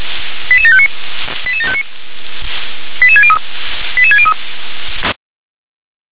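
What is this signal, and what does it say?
Five-tone selective call (ZVEI), heard as quick runs of five tones, tones of about 70 ms. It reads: 88759 88759 88603 88603
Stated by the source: Five-tone ZVEI selective call received by SDR.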